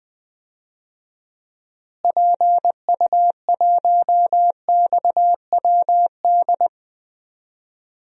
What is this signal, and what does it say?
Morse code, heard as a keyed tone, about 20 wpm, 700 Hz. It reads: PU1XWD
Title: DXCC-100 Call Signs - 6 Characters - Repeated 2x - 20wpm